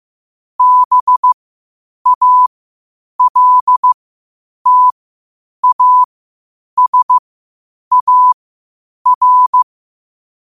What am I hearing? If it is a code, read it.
Morse code, heard as a keyed tone, about 15 wpm, spaced as for 8 wpm, 1000 Hz. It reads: BALTASAR
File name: qotdfiles / 8wpm